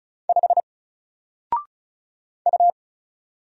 Morse code, heard as a keyed tone, about 36 wpm, 700 Hz. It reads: 5 U
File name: Single-Letter-Number-35wpm